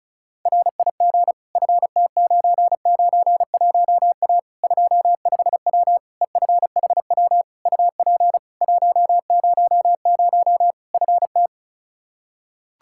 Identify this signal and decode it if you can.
Morse code, heard as a keyed tone, about 36 wpm, 700 Hz. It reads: RIG FT991A 25W EFHW UP 100 FT
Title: Practice QSO - Exchange 2 - Info - 35wpm